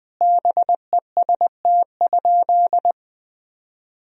Morse code, heard as a keyed tone, about 21 wpm, 700 Hz. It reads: BEST?
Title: Sentences from Top 1000 Words - Review Part 1 - 20wpm